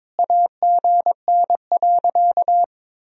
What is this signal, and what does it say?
Morse code, heard as a keyed tone, about 22 wpm, 700 Hz. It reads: AGN.
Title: Sentences from Top 300 Words and Abbreviations 22wpm